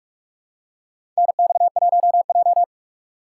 Morse code, heard as a keyed tone, about 44 wpm, 690 Hz. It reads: NX1J
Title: USA Call Signs - 4 Characters - 2x Character Spacing - 45wpm